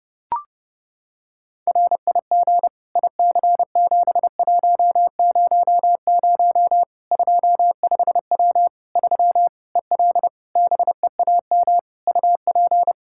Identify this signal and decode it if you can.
Morse code, heard as a keyed tone, about 30 wpm, 700 Hz. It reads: RIG IC7100 25W 3 EL BEAM UP
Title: Practice QSO - Exchange 2 - Info - 30wpm